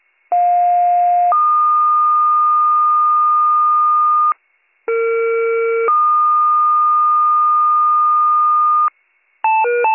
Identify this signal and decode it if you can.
Two-tone page, heard as a one-second tone, then a three-second tone, about 710 then 1220 Hz, about 460 then 1220 Hz. It is heard from 0.3 to 8.9 s.